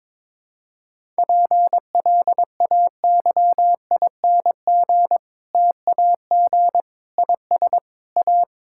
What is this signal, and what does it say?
Morse code, heard as a keyed tone, about 22 wpm, 700 Hz. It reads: PLAYING TAG IS A
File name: Difficult Sentences with Hints - 22wpm